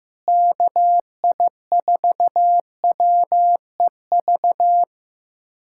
Morse code, heard as a keyed tone, about 15 wpm, 700 Hz. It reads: KI4WEV
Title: USA Callsign DE Callsign - 6 Characters - 15wpm